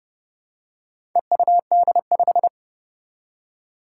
Morse code, heard as a keyed tone, about 30 wpm, 700 Hz. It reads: EUD5
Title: Four Challenging Letter-Number Combinations 30wpm